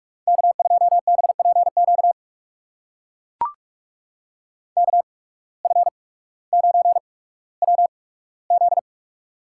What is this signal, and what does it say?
Morse code, heard as a keyed tone, about 44 wpm, spaced as for 19 wpm, 690 Hz. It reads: K2BPQ KF9WZ
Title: USA Call Signs - 5 Characters - 8x Character Spacing - 45wpm